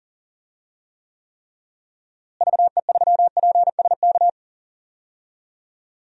Morse code, heard as a keyed tone, about 37 wpm, 700 Hz. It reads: VE3PSK
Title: DXCC-100 Call Signs - 6 Characters - Repeated 3x - 40wpm